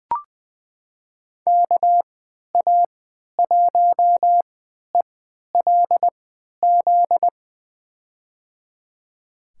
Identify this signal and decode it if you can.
Morse code, heard as a keyed tone, about 20 wpm, spaced as for 11 wpm, 700 Hz. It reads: KA1ELZ
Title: USA Call Signs - 6 Characters - 3x Character Spacing - 20wpm